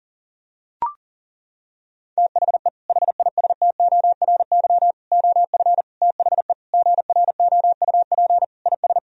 Morse code, heard as a keyed tone, about 40 wpm, 700 Hz. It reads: THE HISTORY OF THE GROUP IS
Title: Sentences from Top 1000 Words - Review Part 1 - 40wpm